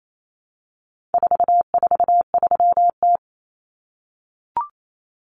Morse code, heard as a keyed tone, about 28 wpm, 700 Hz. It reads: 443T
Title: Common Cut Numbers - 4-Digit 28wpm